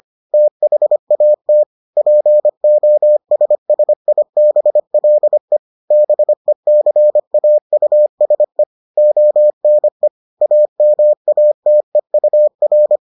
Morse code, heard as a keyed tone, about 25 wpm, 600 Hz. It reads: THAT POSSIBLE BECAUSE ONE AMATEUR